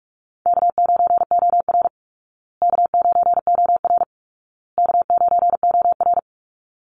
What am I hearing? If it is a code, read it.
Morse code, heard as a keyed tone, about 44 wpm, 700 Hz. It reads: K9OR K9OR K9OR
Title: Call Signs Repeated 2x - 45wpm